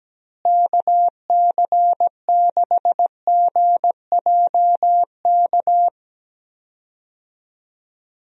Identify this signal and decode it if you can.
Morse code, heard as a keyed tone, about 17 wpm, 700 Hz. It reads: KC6GJK
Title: USA Call Signs - 6 Characters - Repeated 2x - 17wpm